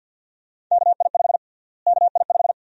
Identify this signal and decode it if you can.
Morse code, heard as a keyed tone, about 52 wpm, 700 Hz. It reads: KI5 KI5
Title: USA Call Sign Prefixes - Repeated 2x - 50wpm